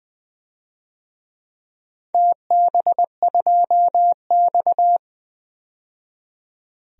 Morse code, heard as a keyed tone, about 20 wpm, 700 Hz. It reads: TB2X
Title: DXCC-100 Call Signs - 4 Characters - 20wpm